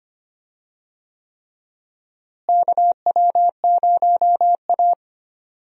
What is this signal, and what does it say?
Morse code, heard as a keyed tone, about 25 wpm, 700 Hz. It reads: KW0A
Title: USA Call Signs - 4 Characters - Repeated 3x - 25wpm